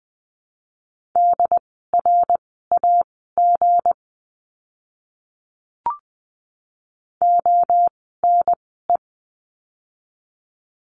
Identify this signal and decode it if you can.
Morse code, heard as a keyed tone, about 19 wpm, spaced as for 14 wpm, 700 Hz. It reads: DRAG ONE
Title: 160 Lesson 26f - Letter g - Words 20wpm